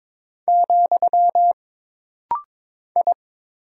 Morse code, heard as a keyed tone, about 22 wpm, 700 Hz. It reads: , I
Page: This rapid-fire timing is geared to push your ICR (Instant Character Recognition) skills.